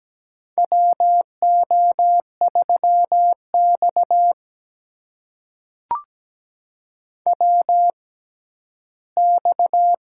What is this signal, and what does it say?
Morse code, heard as a keyed tone, about 17 wpm, spaced as for 11 wpm, 700 Hz. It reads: WO3X WX
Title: USA Call Signs - 4 Characters - 6x Character Spacing - 17wpm